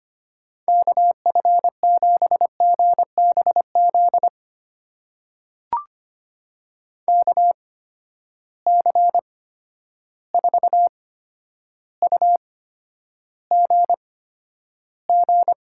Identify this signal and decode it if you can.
Morse code, heard as a keyed tone, about 25 wpm, spaced as for 11 wpm, 700 Hz. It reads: KF7GBZ KC4UGG
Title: USA Call Signs - 6 Characters - 8x Character Spacing - 25wpm